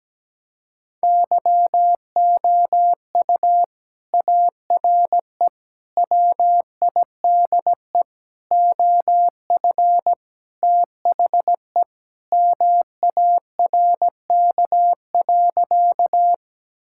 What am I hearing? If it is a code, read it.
Morse code, heard as a keyed tone, about 17 wpm, 700 Hz. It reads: YOU ARE WIDE OF THE MARK.
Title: Sentences from Top 1000 Words - Review Part 6 - 17wpm